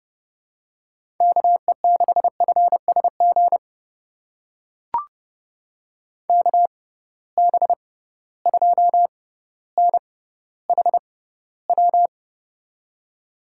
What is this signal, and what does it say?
Morse code, heard as a keyed tone, about 30 wpm, spaced as for 15 wpm, 700 Hz. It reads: KE6FSG KB2NHW
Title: USA Call Signs - 6 Characters - 6x Character Spacing - 30wpm